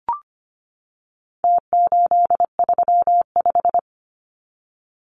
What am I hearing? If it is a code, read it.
Morse code, heard as a keyed tone, about 25 wpm, 700 Hz. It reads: T835
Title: Numbers - Contest Style - 4-Digit 25wpm